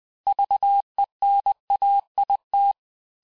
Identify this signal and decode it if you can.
Morse code, heard as a keyed tone, about 20 wpm, 800 Hz. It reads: VENAIT
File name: cours-du-mercredi / audios / cours-870-texte-CLAIR-vitesse-19-MPM